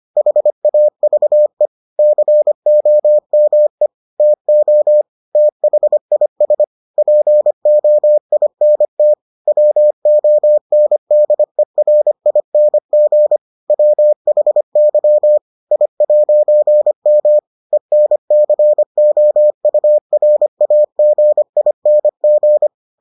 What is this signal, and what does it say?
Morse code, heard as a keyed tone, about 25 wpm, 600 Hz. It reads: HAVE COME TO THIS POINT WONDERING WHY I'M ENCOURAGING